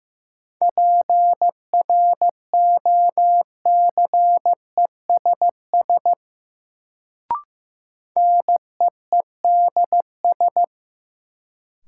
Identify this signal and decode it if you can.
Morse code, heard as a keyed tone, about 15 wpm, 700 Hz. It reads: PROCESS NEEDS
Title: Sentences from Top 1000 Words - Review Part 3 - New Word Drill - 15wpm